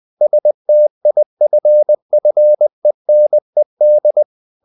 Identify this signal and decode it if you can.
Morse code, heard as a keyed tone, about 20 wpm, 600 Hz. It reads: STIFFENED
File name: the old man and the sea / the old man and the sea part 1 20wpm